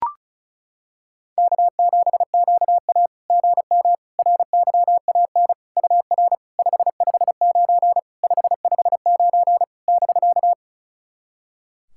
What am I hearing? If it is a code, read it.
Morse code, heard as a keyed tone, about 36 wpm, 700 Hz. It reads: K7QA GM RYAN UR 559 559 <BK>
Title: SOTA QSO Practice 35wpm